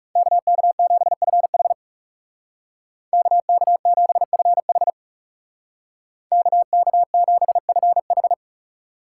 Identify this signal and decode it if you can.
Morse code, heard as a keyed tone, about 35 wpm, 700 Hz. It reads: KK7FH KK7FH KK7FH
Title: Speed Racing - USA Call Signs 30wpm